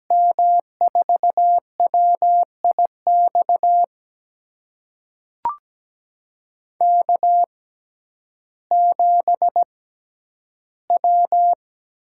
Morse code, heard as a keyed tone, about 17 wpm, spaced as for 10 wpm, 700 Hz. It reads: M4WIX K7W
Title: USA Call Signs - 6x Character Spacing - 17wpm